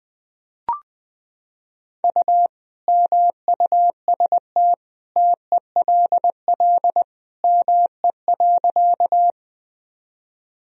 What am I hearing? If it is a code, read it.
Morse code, heard as a keyed tone, about 20 wpm, 700 Hz. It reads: U MUST TELL ME.
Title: Sentences from Top 200 Words and Abbreviations 20wpm